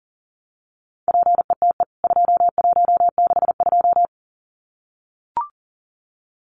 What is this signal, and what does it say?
Morse code, heard as a keyed tone, about 40 wpm, 700 Hz. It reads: PETE 2162
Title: CWT Exchange 40wpm